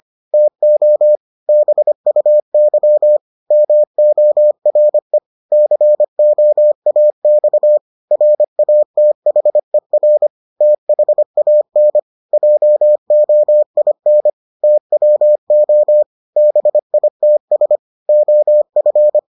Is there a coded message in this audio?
Yes: Morse code from 0.3 to 19.2 s, heard as a keyed tone, about 25 wpm, 600 Hz. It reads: TO BUY MORE COAX RATHER THAN JOIN TWO BITS OF